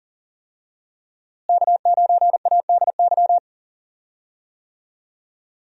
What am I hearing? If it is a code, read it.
Morse code, heard as a keyed tone, about 40 wpm, 700 Hz. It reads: K9ADY
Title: USA Call Signs - 5 Characters - Repeated 3x - 40wpm